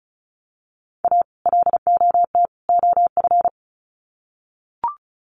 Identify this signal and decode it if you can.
Morse code, heard as a keyed tone, about 34 wpm, 700 Hz. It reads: A LOT OF
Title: Sets of 3 Words 35wpm